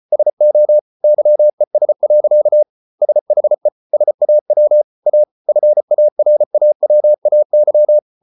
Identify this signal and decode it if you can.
Morse code, heard as a keyed tone, about 34 wpm, 600 Hz. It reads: SO YES. SHE SAW A FARAWAY